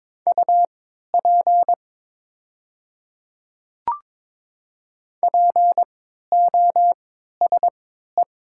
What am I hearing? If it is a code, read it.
Morse code, heard as a keyed tone, about 22 wpm, spaced as for 13 wpm, 700 Hz. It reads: UP POSE
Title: 3x Character Spacing - Top 500 Words - 22wpm